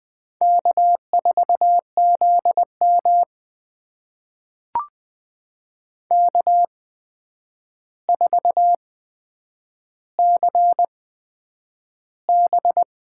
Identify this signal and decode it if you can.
Morse code, heard as a keyed tone, about 20 wpm, spaced as for 9 wpm, 700 Hz. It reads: K4ZM K4CB
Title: USA Call Signs - 4 Characters - 8x Character Spacing - 20wpm